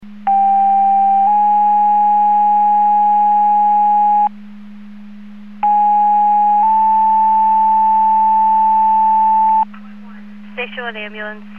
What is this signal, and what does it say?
Two-tone page, heard as a one-second tone, then a three-second tone, about 790 then 830 Hz, about 830 then 880 Hz, with speech. It is heard from 0.3 to 9.6 s.